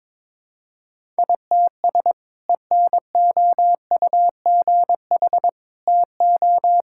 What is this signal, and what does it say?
Morse code, heard as a keyed tone, about 22 wpm, 700 Hz. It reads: ITS ENOUGH TO
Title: Sentences from Top 1000 Words - Review Part 6 - 22wpm